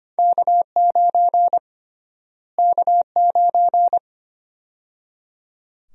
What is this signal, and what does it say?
Morse code, heard as a keyed tone, about 25 wpm, 700 Hz. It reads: K9 K9
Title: USA Call Sign Prefixes - Repeated 2x - 25wpm